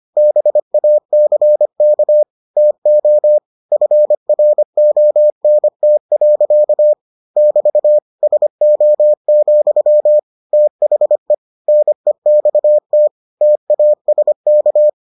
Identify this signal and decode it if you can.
Morse code, heard as a keyed tone, about 25 wpm, 600 Hz. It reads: BACK TO FRONT. <BT> SO, THE NEXT TASK